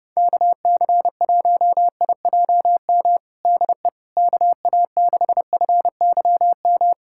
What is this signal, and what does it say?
Morse code, heard as a keyed tone, about 31 wpm, 700 Hz. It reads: KC1IJM DE KA6FYM
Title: USA Callsign DE Callsign - 6 Characters - 30wpm